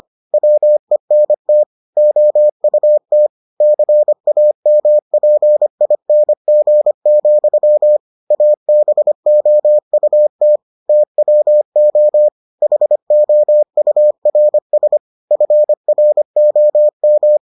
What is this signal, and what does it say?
Morse code, heard as a keyed tone, about 25 wpm, 600 Hz. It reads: WENT OUT CAMPING, ABOUT TWO HOURS FROM